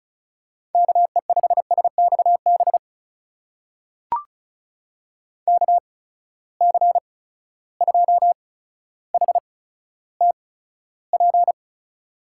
Morse code, heard as a keyed tone, about 36 wpm, spaced as for 15 wpm, 700 Hz. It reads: KE5SXB KC2HTP
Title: USA Call Signs - 6 Characters - 8x Character Spacing - 35wpm